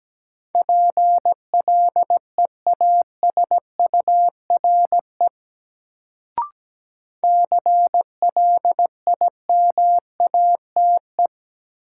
Morse code, heard as a keyed tone, about 17 wpm, 700 Hz. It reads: PLEASURE CLIMATE
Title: Sentences from Top 1600 Words - New Word Drill Alt - 17wpm